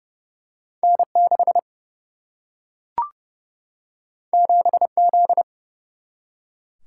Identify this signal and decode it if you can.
Morse code, heard as a keyed tone, about 30 wpm, 700 Hz. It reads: N 6 7 Z
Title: Single-Letter Single-Number 30wpm